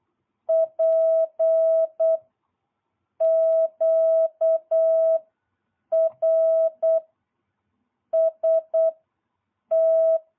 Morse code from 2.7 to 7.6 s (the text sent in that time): Q R